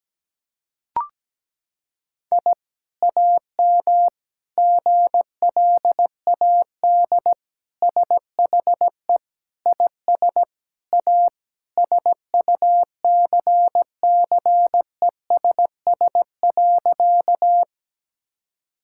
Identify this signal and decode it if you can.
Morse code, heard as a keyed tone, about 17 wpm, 700 Hz. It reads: I AM GLAD SHE IS A SUCCESS.
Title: Sentences from Top 1000 Words - Review Part 6 - 17wpm